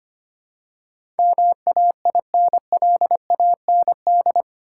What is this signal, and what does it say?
Morse code, heard as a keyed tone, about 25 wpm, 690 Hz. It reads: MAINLAND
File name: 8 Letter Words - First 3 Letters Given - 25wpm